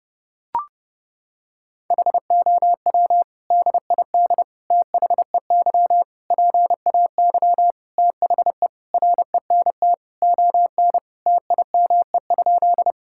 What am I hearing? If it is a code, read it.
Morse code, heard as a keyed tone, about 30 wpm, 700 Hz. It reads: HOW DID THEY PAY THE RENT ON TIME?
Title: Sentences from Top 1000 Words - Review Part 5 - 30wpm